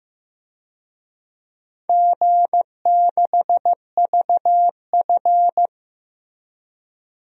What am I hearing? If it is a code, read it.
Morse code, heard as a keyed tone, about 15 wpm, 700 Hz. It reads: G6VF
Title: DXCC-100 Call Signs - 4 Characters - Repeated 3x - 15wpm